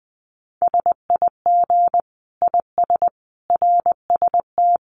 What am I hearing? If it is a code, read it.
Morse code, heard as a keyed tone, about 20 wpm, 700 Hz. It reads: SIG IS RST
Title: Common QSO Phrases 20wpm